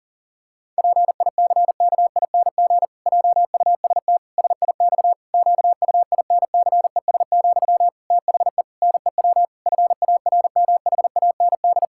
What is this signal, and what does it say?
Morse code, heard as a keyed tone, about 37 wpm, 700 Hz. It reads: PICKING JUST SIX QUINCES, THE NEW FARMHAND
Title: Pangram Challenge v2 - 40wpm